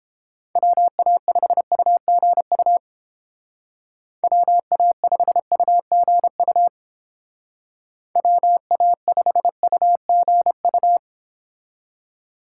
Morse code, heard as a keyed tone, about 29 wpm, 700 Hz. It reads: WA5UGU WA5UGU WA5UGU